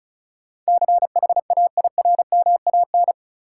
Morse code, heard as a keyed tone, about 36 wpm, 700 Hz. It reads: CHAIRMAN
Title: Sets of 3 Words - New Word Drill - 35wpm